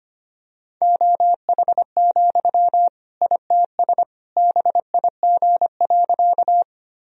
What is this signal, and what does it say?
Morse code, heard as a keyed tone, about 25 wpm, 700 Hz. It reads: OH, ITS BIG.
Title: Sentences from Top 1000 Words - Review Part 1 - 25wpm